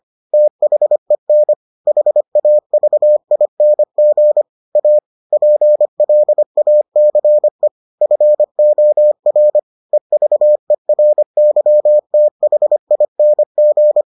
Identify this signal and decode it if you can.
Morse code, heard as a keyed tone, about 25 wpm, 600 Hz. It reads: THEN HAVING A PLACE FOR EVERYTHING